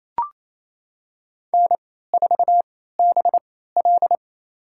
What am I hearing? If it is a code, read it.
Morse code, heard as a keyed tone, about 28 wpm, spaced as for 16 wpm, 700 Hz. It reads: N4BL
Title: USA Call Signs - 4 Characters - 3x Character Spacing - 28wpm